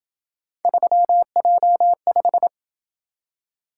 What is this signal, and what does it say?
Morse code, heard as a keyed tone, about 26 wpm, 700 Hz. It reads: 3J5